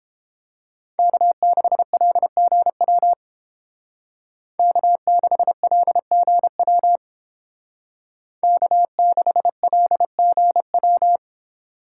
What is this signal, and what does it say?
Morse code, heard as a keyed tone, about 29 wpm, 700 Hz. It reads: K6LGW K6LGW K6LGW